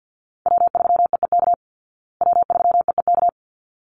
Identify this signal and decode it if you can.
Morse code, heard as a keyed tone, about 50 wpm, 700 Hz. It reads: W3EEK W3EEK
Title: Call Signs Repeated 3x - 50wpm